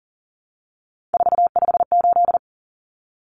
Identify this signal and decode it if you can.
Morse code, heard as a keyed tone, about 40 wpm, 700 Hz. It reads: 458
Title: Numbers - 3-Digit 40wpm